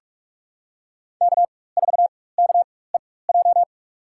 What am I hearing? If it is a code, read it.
Morse code, heard as a keyed tone, about 44 wpm, spaced as for 21 wpm, 700 Hz. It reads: K4KEJ
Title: USA Call Signs - 5 Characters - 4x Character Spacing - 45wpm